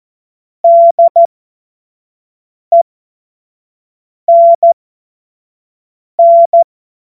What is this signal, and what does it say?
Morse code, heard as a keyed tone, about 14 wpm, spaced as for 5 wpm, 680 Hz. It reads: DENN
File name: Morsekurs / Witze / Witz 14. Februar 2023 25 F70